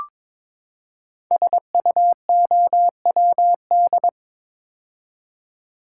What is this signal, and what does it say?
Morse code, heard as a keyed tone, about 22 wpm, 700 Hz. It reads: SUOWD
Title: Code Groups - 5 Letters - 22wpm